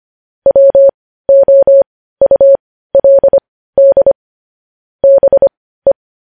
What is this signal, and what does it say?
Morse code, heard as a keyed tone, about 25 wpm, spaced as for 15 wpm, 550 Hz. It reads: WOULD BE
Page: news headlines in morse code at 15 wpm